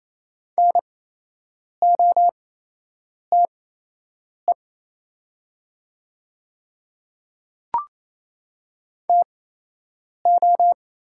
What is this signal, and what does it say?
Morse code, heard as a keyed tone, about 28 wpm, spaced as for 8 wpm, 700 Hz. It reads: NOTE TO
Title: 8x Character Spacing - Top 500 Words - 28wpm